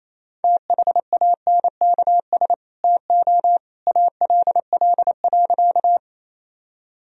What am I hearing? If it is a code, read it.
Morse code, heard as a keyed tone, about 27 wpm, 700 Hz. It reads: THANKS TO ALL.
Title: Sentences from Top 1000 Words - Review Part 3 - 28wpm